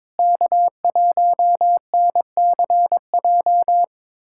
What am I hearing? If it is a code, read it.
Morse code, heard as a keyed tone, about 22 wpm, 700 Hz. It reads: K1NCJ
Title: USA Callsign DE Callsign - 5 Characters - 22wpm